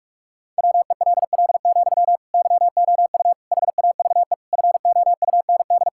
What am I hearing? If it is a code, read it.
Morse code, heard as a keyed tone, about 46 wpm, 700 Hz. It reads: WELL, YOU HAVE FOUND
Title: Sentences from Top 1000 Words - Review Part 6 - 45wpm